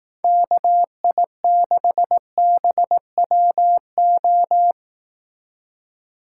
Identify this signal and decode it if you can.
Morse code, heard as a keyed tone, about 17 wpm, 700 Hz. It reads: KI6BWO
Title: Speed Racing - USA 2x3 Call Signs 15wpm